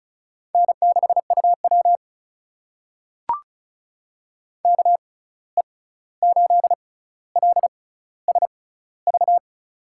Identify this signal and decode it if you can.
Morse code, heard as a keyed tone, about 36 wpm, spaced as for 16 wpm, 700 Hz. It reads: N6UW KE8LSV
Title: USA Call Signs - 6x Character Spacing - 35wpm